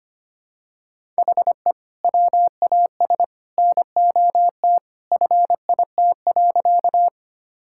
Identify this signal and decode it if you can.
Morse code, heard as a keyed tone, about 25 wpm, 700 Hz. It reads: HE WAS NOT FIT.
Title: Sentences from Top 1000 Words - Review Part 5 - 25wpm